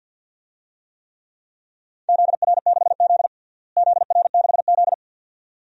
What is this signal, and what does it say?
Morse code, heard as a keyed tone, about 48 wpm, 700 Hz. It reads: ZR6Z ZR6Z
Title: DXCC-100 Call Signs - 4 Characters - Repeated 2x - 50wpm